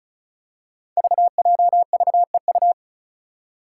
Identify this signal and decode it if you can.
Morse code, heard as a keyed tone, about 36 wpm, 700 Hz. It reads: VJVEU
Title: Five Challenging Letter-Number Combinations 35wpm